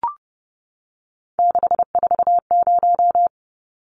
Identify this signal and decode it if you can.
Morse code, heard as a keyed tone, about 30 wpm, 700 Hz. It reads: 640
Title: Numbers - 3-Digit 30wpm